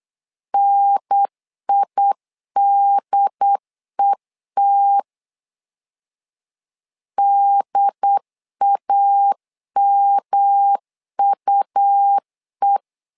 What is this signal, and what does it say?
Morse code, heard as a keyed tone, about 8 wpm, 800 Hz. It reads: NIDET DAMUE
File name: LEZIONI / 3 MWRO / 3 M 8wpm 2300